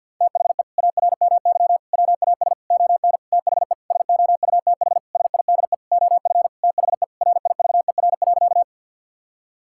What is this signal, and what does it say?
Morse code, heard as a keyed tone, about 47 wpm, 700 Hz. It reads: THE ARMY WAS ON THE SOUTH SIDE OF THE RIVER.
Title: Sentences from Top 1000 Words - Review Part 5 - 50wpm